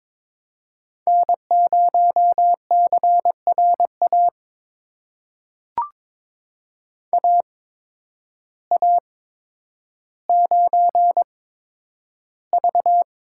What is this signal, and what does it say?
Morse code, heard as a keyed tone, about 22 wpm, spaced as for 10 wpm, 700 Hz. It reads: N0CRA AA9V
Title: USA Call Signs - 5 Characters - 8x Character Spacing - 22wpm